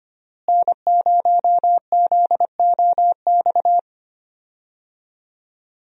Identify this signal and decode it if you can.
Morse code, heard as a keyed tone, about 25 wpm, 700 Hz. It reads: N0ZOX
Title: USA Call Signs - 5 Characters - Repeated 2x - 25wpm